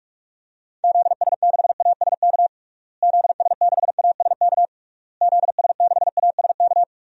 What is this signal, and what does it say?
Morse code, heard as a keyed tone, about 46 wpm, 700 Hz. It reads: ZS6ASK ZS6ASK ZS6ASK
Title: DXCC-100 Call Signs - 6 Characters - Repeated 3x - 45wpm